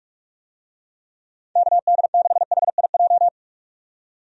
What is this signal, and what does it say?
Morse code, heard as a keyed tone, about 44 wpm, 700 Hz. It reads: KD6HIJ
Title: USA Call Signs - 6 Characters - 45wpm